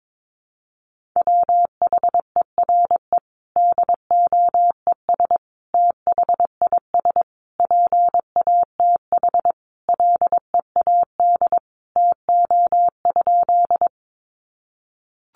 Morse code, heard as a keyed tone, about 22 wpm, 700 Hz. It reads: WHERE DOES THIS PATH LEAD TO?
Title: Sentences from Top 1400 Words 22wpm